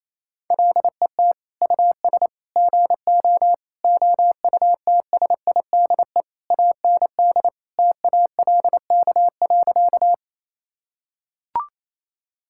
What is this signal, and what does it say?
Morse code, heard as a keyed tone, about 28 wpm, 700 Hz. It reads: LET US GO OUTSIDE AND TALK.
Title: Sentences from Top 1000 Words - Review Part 6 - 28wpm